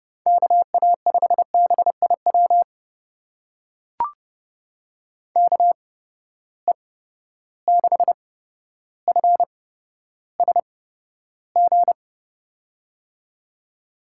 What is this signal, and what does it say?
Morse code, heard as a keyed tone, about 30 wpm, spaced as for 13 wpm, 700 Hz. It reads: KA5BIW KE6FSG
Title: USA Call Signs - 6 Characters - 8x Character Spacing - 30wpm